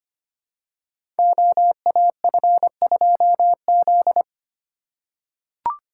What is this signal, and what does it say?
Morse code, heard as a keyed tone, about 25 wpm, 700 Hz. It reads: OAF2Z
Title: Code Groups - 5 Letter:Numbers - 25wpm